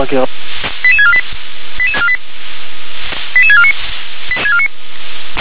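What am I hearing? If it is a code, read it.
Five-tone selective call (ZVEI), heard as quick runs of five tones, tones of about 70 ms, with speech. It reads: 88648 88649
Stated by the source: ZVEI five-tone call received on an SDR.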